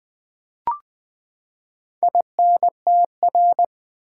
Morse code, heard as a keyed tone, about 20 wpm, 700 Hz. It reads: INTR
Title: Build A Word - 4 Letters At A Time - 12-Letter Words - 20wpm